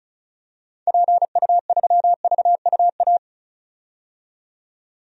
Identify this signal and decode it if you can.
Morse code, heard as a keyed tone, about 34 wpm, 700 Hz. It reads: PU3VUA
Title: DXCC-100 Call Signs - 6 Characters - 35wpm